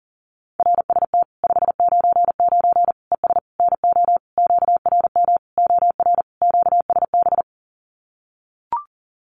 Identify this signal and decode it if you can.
Morse code, heard as a keyed tone, about 40 wpm, 700 Hz. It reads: RST 599 ES NO QRM OR QSB
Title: Common QSO Phrases 40wpm